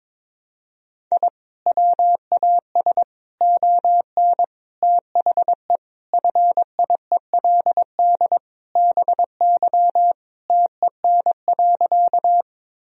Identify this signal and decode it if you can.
Morse code, heard as a keyed tone, about 22 wpm, 700 Hz. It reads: I WAS ON THE FIELD BY TEN.
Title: Sentences from Top 1000 Words - Review Part 2 - 22wpm